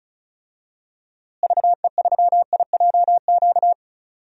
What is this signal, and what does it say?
Morse code, heard as a keyed tone, about 34 wpm, 700 Hz. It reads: VE3IJQ
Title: DXCC-100 Call Signs - 6 Characters - Repeated 2x - 35wpm